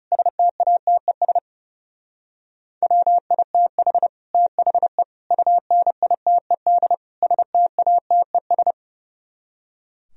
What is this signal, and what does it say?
Morse code, heard as a keyed tone, about 39 wpm, 700 Hz. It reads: STATES WITH THE UNITED STATES